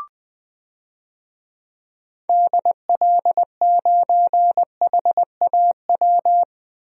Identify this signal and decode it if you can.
Morse code, heard as a keyed tone, about 20 wpm, 700 Hz. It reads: DL9HAW
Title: DXCC-100 Call Signs - 6 Characters - Repeated 3x - 20wpm